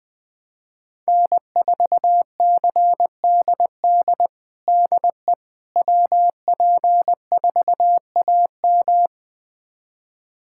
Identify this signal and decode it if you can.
Morse code, heard as a keyed tone, about 20 wpm, 700 Hz. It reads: N4CDD DE WP4AM
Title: USA Callsign DE Callsign - 5 Characters - 20wpm